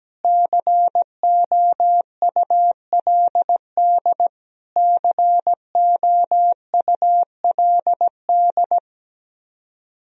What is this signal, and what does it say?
Morse code, heard as a keyed tone, about 17 wpm, 700 Hz. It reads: COULD COULD
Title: Top 100 Words Repeated 17wpm